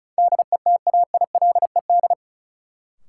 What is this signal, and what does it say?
Morse code, heard as a keyed tone, about 36 wpm, 700 Hz. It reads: DETAILED
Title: Sentences from Top 1500 Words - New Word Drill Alt - 35wpm